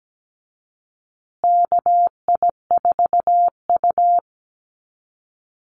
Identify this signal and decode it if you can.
Morse code, heard as a keyed tone, about 17 wpm, 700 Hz. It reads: KI4U
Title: USA Call Signs Repeated 3x 17wpm